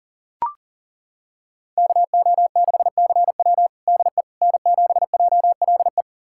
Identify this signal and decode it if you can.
Morse code, heard as a keyed tone, about 40 wpm, 700 Hz. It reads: KO6CW DE N7JLE
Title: USA Callsign DE Callsign - 5 Characters - 40wpm